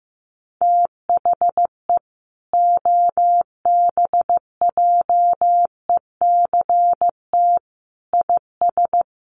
Morse code, heard as a keyed tone, about 15 wpm, 700 Hz. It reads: THE OBJECT IS
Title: Sentences from Top 1300 Words 15wpm